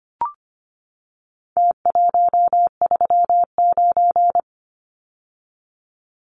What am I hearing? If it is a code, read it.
Morse code, heard as a keyed tone, about 25 wpm, 700 Hz. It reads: T139
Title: Numbers - Contest Style - 4-Digit 25wpm